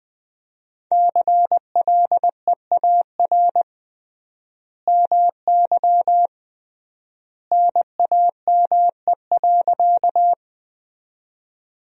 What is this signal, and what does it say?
Morse code, heard as a keyed tone, about 21 wpm, 700 Hz. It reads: CLEAR MY NAME.
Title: 3x Word-Spacing - Sentences from Top 500 Words - 20wpm